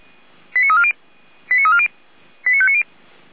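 Five-tone selective call (ZVEI), heard as quick runs of five tones, tones of about 70 ms. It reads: 89340 89340 89590
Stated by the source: ZVEI five-tone selective call 89315 received on an SDR by feuerwehr-orlamuende, a fire brigade at Orlamünde.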